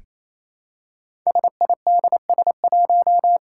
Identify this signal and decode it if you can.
Morse code, heard as a keyed tone, about 27 wpm, 700 Hz. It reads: SIDS1
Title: Five Challenging Letter-Number Combinations 28wpm